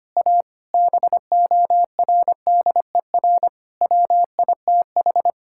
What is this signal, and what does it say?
Morse code, heard as a keyed tone, about 25 wpm, 700 Hz. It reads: A BORDER WITH
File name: practice / 25wpm / Sentences with 4-Word Phrases - 25wpm